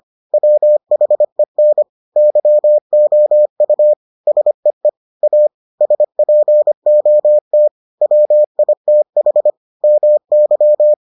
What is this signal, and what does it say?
Morse code, heard as a keyed tone, about 25 wpm, 600 Hz. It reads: WHEN YOU SEE A SPOT WITH MY